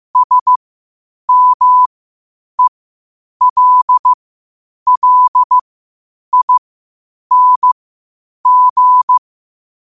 Morse code, heard as a keyed tone, about 15 wpm, spaced as for 8 wpm, 1000 Hz. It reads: SMELLING
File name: qotdfiles / 8wpm